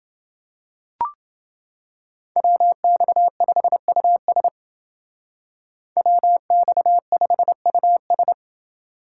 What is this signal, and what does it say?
Morse code, heard as a keyed tone, about 29 wpm, 700 Hz. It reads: WX5US WX5US